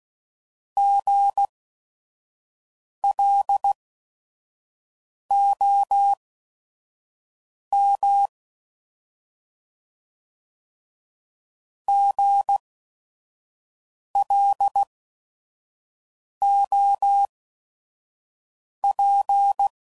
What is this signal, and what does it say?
Morse code, heard as a keyed tone, about 16 wpm, spaced as for 5 wpm, 800 Hz. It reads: GLOM GLOP
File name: Prefixes / By First Letter / Gs